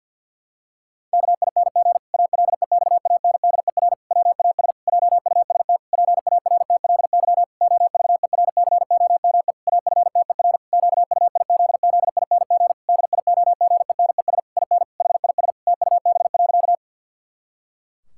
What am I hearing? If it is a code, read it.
Morse code, heard as a keyed tone, about 49 wpm, 700 Hz. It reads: KING ALEXANDER WAS JUST PARTLY OVERCOME AFTER QUIZZING DIOGENES IN HIS TUB.
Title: Pangram Challenge v2 - 50wpm